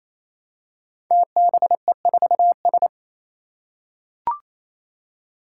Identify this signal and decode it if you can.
Morse code, heard as a keyed tone, about 27 wpm, 700 Hz. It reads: TBE4S
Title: Five Challenging Letter-Number Combinations 28wpm